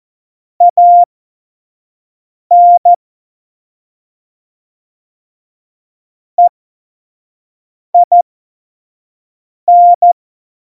Morse code, heard as a keyed tone, about 14 wpm, spaced as for 5 wpm, 700 Hz. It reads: AN EIN